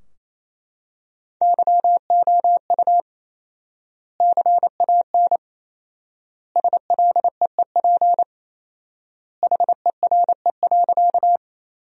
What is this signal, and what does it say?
Morse code, heard as a keyed tone, about 28 wpm, 700 Hz. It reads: YOU CAN SLEEP HERE.
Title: 4x Word-Spacing - Sentences from Top 500 Words - 28wpm